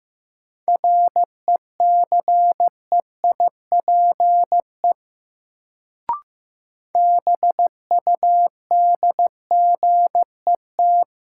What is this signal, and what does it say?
Morse code, heard as a keyed tone, about 15 wpm, 700 Hz. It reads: RECEIPE BUDGET
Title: Sentences from Top 1200 Words - New Word Drill Alt - 15wpm